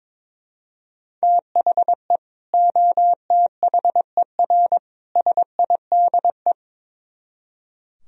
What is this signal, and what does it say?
Morse code, heard as a keyed tone, about 24 wpm, 700 Hz. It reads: THE OTHER SIDE
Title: Speed-Racing - Sets of 3 Words 22wpm